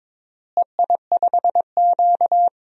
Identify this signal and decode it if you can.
Morse code, heard as a keyed tone, about 22 wpm, 700 Hz. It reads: EI5Q
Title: DXCC-100 Call Signs - 4 Characters - Repeated 3x - 22wpm